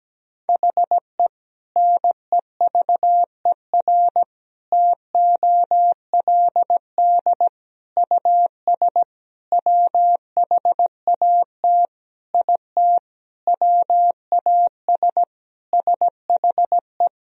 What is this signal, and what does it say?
Morse code, heard as a keyed tone, about 17 wpm, 700 Hz. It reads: HE NEVER TOLD US WHAT IT WAS SHE
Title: Sentences from Top 400 Words and Abbreviations 17wpm